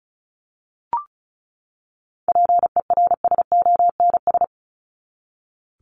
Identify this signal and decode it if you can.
Morse code, heard as a keyed tone, about 34 wpm, 700 Hz. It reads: PERSONS
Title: ES:S Suffix - Top 100 Words with Base Words 35wpm